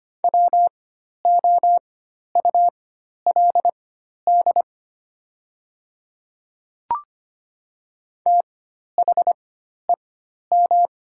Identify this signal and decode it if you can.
Morse code, heard as a keyed tone, about 25 wpm, spaced as for 12 wpm, 700 Hz. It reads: WOULD THEM
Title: 4x Character Spacing - Top 100 Words - 25wpm